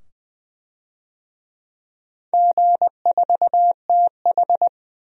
Morse code, heard as a keyed tone, about 20 wpm, 700 Hz. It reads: G4TH
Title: DXCC-100 Call Signs - 4 Characters - 20wpm